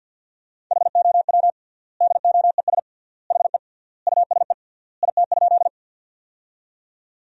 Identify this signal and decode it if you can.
Morse code, heard as a keyed tone, about 50 wpm, 700 Hz. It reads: HOW DOES HE USE IT?
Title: 3x Word-Spacing - Sentences from Top 200 Words - 50wpm